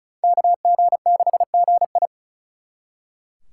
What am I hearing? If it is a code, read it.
Morse code, heard as a keyed tone, about 45 wpm, 700 Hz. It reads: KG6GI